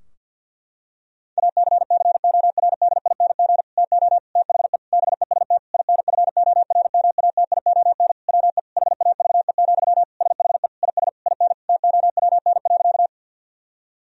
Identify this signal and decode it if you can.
Morse code, heard as a keyed tone, about 50 wpm, 700 Hz. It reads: ACCORDING TO THE BEST INFORMATION WE HAVE, SHE IS IN TOWN.
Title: Sentences from Top 1000 Words - Review Part 4 - 50wpm